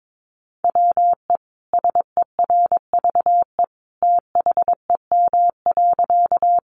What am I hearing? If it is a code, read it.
Morse code, heard as a keyed tone, about 22 wpm, 700 Hz. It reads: WE SERVE THEM.
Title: Sentences from Top 400 Words 22wpm